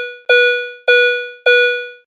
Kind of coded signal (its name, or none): none